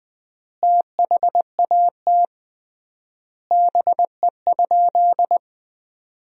Morse code, heard as a keyed tone, about 19 wpm, 700 Hz. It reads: THAT BE?
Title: 3x Word-Spacing - Sentences from Top 100 Words - 20wpm